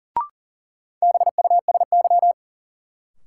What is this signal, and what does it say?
Morse code, heard as a keyed tone, about 40 wpm, 700 Hz. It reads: BUSY
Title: Sentences from Top 1000 Words - Review Part 2 - New Word Drill - 40wpm